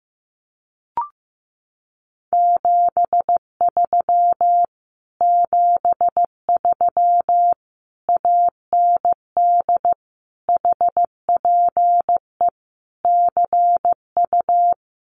Morse code, heard as a keyed tone, about 15 wpm, 700 Hz. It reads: 73 73 AND HPE CU
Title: Common QSO Phrases Encore 15wpm